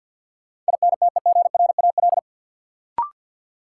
Morse code, heard as a keyed tone, about 47 wpm, 700 Hz. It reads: INTEGRAL
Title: Sets of 5 Words - New Word Drill - 50wpm